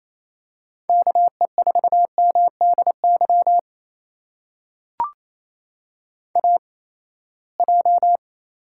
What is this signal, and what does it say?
Morse code, heard as a keyed tone, about 28 wpm, spaced as for 19 wpm, 700 Hz. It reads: KE4MDY AJ
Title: USA Call Signs - 8x Character Spacing - 28wpm